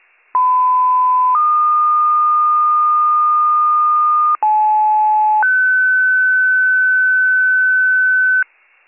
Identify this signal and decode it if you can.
Two-tone page, heard as a one-second tone, then a three-second tone, about 1010 then 1220 Hz, about 860 then 1500 Hz, with noise. It is heard from 0.3 to 8.4 s.